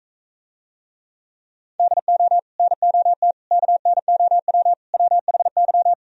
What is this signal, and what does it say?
Morse code, heard as a keyed tone, about 43 wpm, 700 Hz. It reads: DO NOT KNOW WHY